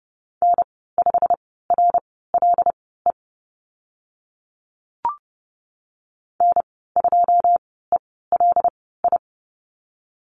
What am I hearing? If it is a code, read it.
Morse code, heard as a keyed tone, about 30 wpm, spaced as for 17 wpm, 700 Hz. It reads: N5RLE N2ELI
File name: Lesson 15g - Number 5 - Call Signs 30wpm